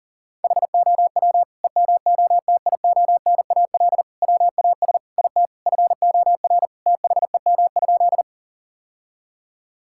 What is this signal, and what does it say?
Morse code, heard as a keyed tone, about 40 wpm, 700 Hz. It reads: HOW EMOTIONAL WAS IT FOR THEM?
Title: Sentences from Top 1000 Words - Review Part 5 - 40wpm